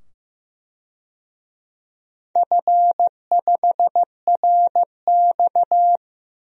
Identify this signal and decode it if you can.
Morse code, heard as a keyed tone, about 15 wpm, 700 Hz. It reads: F5RX
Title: DXCC-100 Call Signs - 4 Characters - 15wpm